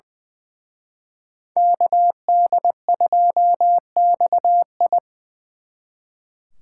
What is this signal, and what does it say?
Morse code, heard as a keyed tone, about 25 wpm, 700 Hz. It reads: KD2XI
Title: Speed Racing - USA Call Signs 20wpm